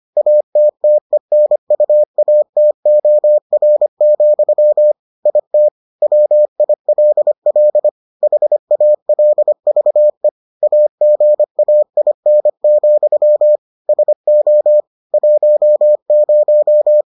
Morse code, heard as a keyed tone, about 25 wpm, 600 Hz. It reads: ATTENUATOR, IT WILL HALVE AGAIN, SO 10